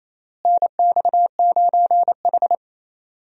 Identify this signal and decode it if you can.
Morse code, heard as a keyed tone, about 27 wpm, 700 Hz. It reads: NX9H
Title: USA Call Signs - 4 Characters - Repeated 3x - 28wpm